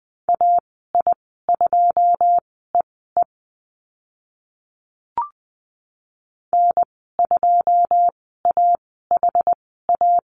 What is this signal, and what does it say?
Morse code, heard as a keyed tone, about 20 wpm, spaced as for 15 wpm, 700 Hz. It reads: AI2EE N2AHA
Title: 088 Lesson 15g - Number 5 - Call Signs 20wpm